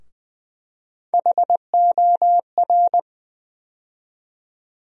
Morse code, heard as a keyed tone, about 20 wpm, 700 Hz. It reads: HOR
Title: USA Call Sign Postfixes - 3 Letters 20wpm